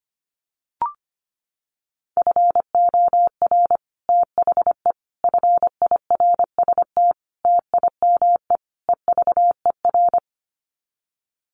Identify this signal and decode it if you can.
Morse code, heard as a keyed tone, about 25 wpm, 700 Hz. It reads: FOR THE FIRST TIME EVER
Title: Sets of 5 Words Encore2 25wpm